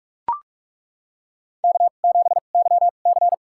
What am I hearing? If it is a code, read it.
Morse code, heard as a keyed tone, about 42 wpm, spaced as for 31 wpm, 700 Hz. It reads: K7YC
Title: USA Call Signs - 4 Characters - 2x Character Spacing - 45wpm